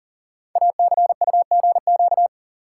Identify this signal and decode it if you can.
Morse code, heard as a keyed tone, about 41 wpm, 690 Hz. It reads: ACUGQ
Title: Code Groups - 5 Letters - 40wpm